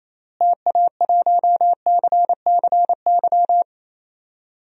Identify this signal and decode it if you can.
Morse code, heard as a keyed tone, about 28 wpm, 700 Hz. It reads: TA1CCY
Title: DXCC-100 Call Signs - 6 Characters - Repeated 2x - 28wpm